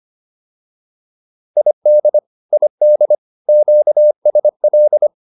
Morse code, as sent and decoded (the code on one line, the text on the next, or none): .. -.. / .. -.. / --.- ... .-..
ID ID QSL